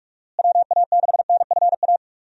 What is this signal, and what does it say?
Morse code, heard as a keyed tone, about 41 wpm, 700 Hz. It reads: WA6NFA
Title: USA Call Signs - 6 Characters - Repeated 2x - 45wpm